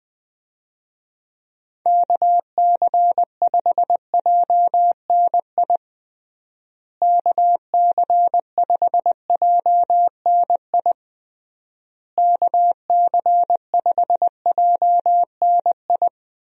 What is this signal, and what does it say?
Morse code, heard as a keyed tone, about 20 wpm, 700 Hz. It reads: KC5JNI KC5JNI KC5JNI
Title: USA Call Signs - 6 Characters - Repeated 3x - 20wpm